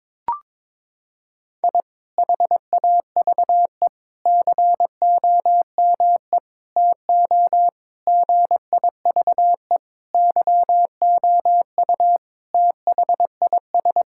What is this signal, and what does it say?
Morse code, heard as a keyed tone, about 22 wpm, 700 Hz. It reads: I HAVE COME TO GIVE YOU THIS
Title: Sentences from Top 1000 Words - Review Part 6 - 22wpm